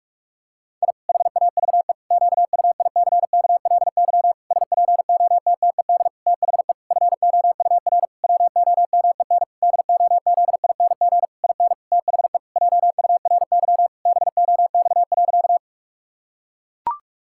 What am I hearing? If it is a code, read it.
Morse code, heard as a keyed tone, about 41 wpm, 700 Hz. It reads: I HAVE QUICKLY SPOTTED THE FOUR WOMEN DOZING IN THE JURY BOX.
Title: Pangram Challenge v2 - 45wpm